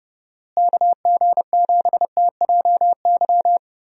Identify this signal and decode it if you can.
Morse code, heard as a keyed tone, about 30 wpm, 700 Hz. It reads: KG7TJY
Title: USA Call Signs - 6 Characters - Repeated 3x - 30wpm